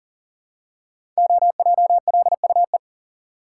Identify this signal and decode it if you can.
Morse code, heard as a keyed tone, about 38 wpm, 700 Hz. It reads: OJLUE
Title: Code Groups - 5 Letter:Numbers - 40wpm